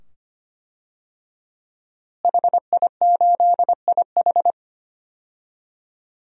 Morse code, heard as a keyed tone, about 25 wpm, 700 Hz. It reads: HI8IH
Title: DXCC-100 Call Signs - 5 Characters - Repeated 2x - 25wpm